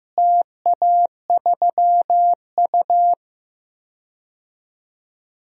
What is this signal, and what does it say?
Morse code, heard as a keyed tone, about 15 wpm, 700 Hz. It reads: TA3U
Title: DXCC-100 Call Signs - 4 Characters - Repeated 2x - 15wpm